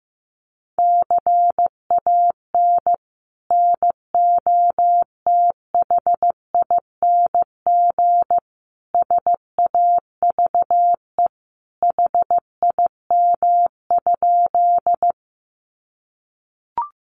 Morse code, heard as a keyed tone, about 15 wpm, 700 Hz. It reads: CAN NOTHING SAVE HIM?
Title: Sentences from Top 1100 Words 15wpm